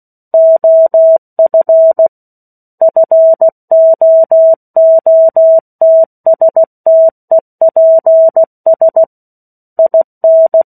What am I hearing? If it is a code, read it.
Morse code, heard as a keyed tone, about 16 wpm, 650 Hz. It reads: OF FOOTSTEPS IN
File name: 03-Chapter-03 16 WPM